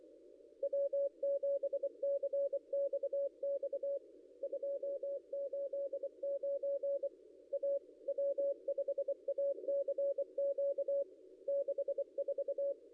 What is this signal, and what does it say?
Morse code, heard as a keyed tone, about 25 wpm, 530 Hz. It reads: W7CXX 289 A W5ACQ 64
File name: contests / ss-cw-k5tr-op-n5rz -0800